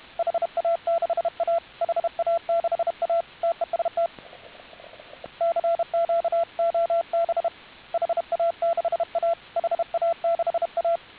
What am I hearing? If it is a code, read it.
Morse code, heard as a keyed tone, about 31 wpm, 670 Hz, with radio noise. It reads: HA6A HA6A TEST CQ OB HA6A HA6A